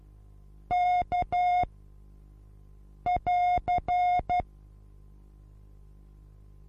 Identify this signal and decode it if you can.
Morse code, heard as a keyed tone, about 12 wpm, 720 Hz. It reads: K <AR>
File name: repos / Morsekurs / DJ4UF Morsen23